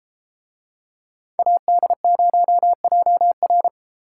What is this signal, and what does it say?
Morse code, heard as a keyed tone, about 33 wpm, 700 Hz. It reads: AD0JR